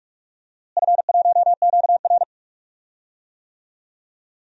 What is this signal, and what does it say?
Morse code, heard as a keyed tone, about 46 wpm, 700 Hz. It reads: F1QR
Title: DXCC-100 Call Signs - 4 Characters - Repeated 3x - 45wpm